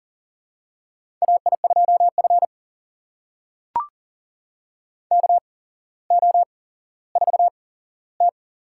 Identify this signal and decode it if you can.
Morse code, heard as a keyed tone, about 43 wpm, spaced as for 17 wpm, 700 Hz. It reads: AI2F KO4T
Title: USA Call Signs - 8x Character Spacing - 40wpm